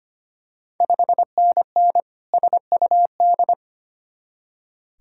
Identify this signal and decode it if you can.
Morse code, heard as a keyed tone, about 25 wpm, 700 Hz. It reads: 5NN SUD
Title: Ontario QSO Party Exchange 25wpm